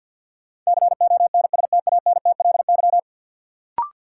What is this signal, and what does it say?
Morse code, heard as a keyed tone, about 53 wpm, 700 Hz. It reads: CONSTANTLY
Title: Sentences from Top 1000 Words - Review Part 7 - New Word Drill - 50wpm